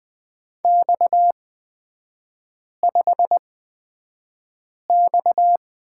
Morse code, heard as a keyed tone, about 20 wpm, spaced as for 5 wpm, 690 Hz. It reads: X5X
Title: 229 Lesson 37e - Letter x - Three Characters 20wpm